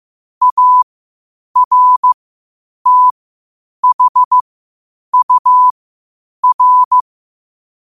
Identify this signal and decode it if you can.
Morse code, heard as a keyed tone, about 15 wpm, spaced as for 8 wpm, 1000 Hz. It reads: ARTHUR
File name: qotdfiles / 8wpm